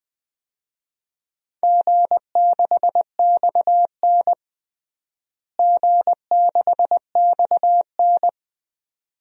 Morse code, heard as a keyed tone, about 20 wpm, 700 Hz. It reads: G6XN G6XN
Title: DXCC-100 Call Signs - 4 Characters - Repeated 2x - 20wpm